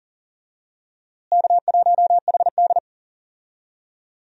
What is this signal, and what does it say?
Morse code, heard as a keyed tone, about 40 wpm, 700 Hz. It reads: K1HD
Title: USA Call Signs - 4 Characters - Repeated 3x - 40wpm